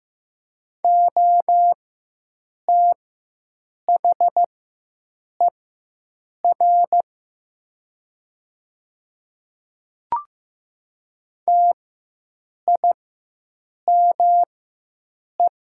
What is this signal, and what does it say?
Morse code, heard as a keyed tone, about 15 wpm, spaced as for 7 wpm, 700 Hz. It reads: OTHER TIME
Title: 4x Character Spacing - Top 100 Words - 15wpm